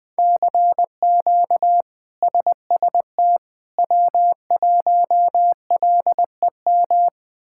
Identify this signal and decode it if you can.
Morse code, heard as a keyed tone, about 20 wpm, 700 Hz. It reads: CQ SST W1LEM
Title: K1USN SST QSO Practice - 20wpm